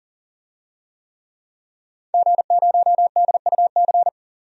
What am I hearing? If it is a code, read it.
Morse code, heard as a keyed tone, about 40 wpm, 700 Hz. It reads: G0DUC
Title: DXCC-100 Call Signs - 5 Characters - 40wpm